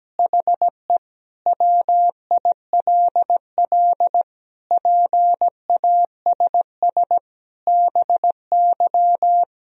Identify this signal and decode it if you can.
Morse code, heard as a keyed tone, about 17 wpm, 700 Hz. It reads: HE WILL PASS BY
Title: Sentences from Top 1000 Words - Review Part 1 - 17wpm